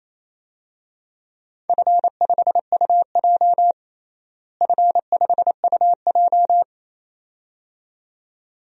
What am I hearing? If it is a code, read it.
Morse code, heard as a keyed tone, about 28 wpm, 700 Hz. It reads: F5UJ F5UJ
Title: DXCC-100 Call Signs - 4 Characters - Repeated 2x - 28wpm